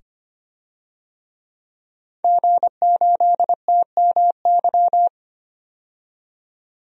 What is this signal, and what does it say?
Morse code, heard as a keyed tone, about 25 wpm, 690 Hz. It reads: G8TMY
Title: DXCC-100 Call Signs - 5 Characters - 25wpm